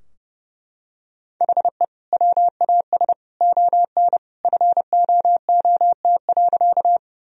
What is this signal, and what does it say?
Morse code, heard as a keyed tone, about 32 wpm, 700 Hz. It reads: HE WAS ON FOOT.
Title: Sentences from Top 1000 Words - Review Part 3 - 30wpm